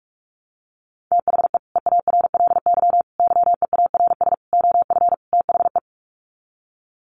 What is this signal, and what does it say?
Morse code, heard as a keyed tone, about 44 wpm, 700 Hz. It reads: THE EARLY YEARS OF THE
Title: Sets of 5 Words Encore 45wpm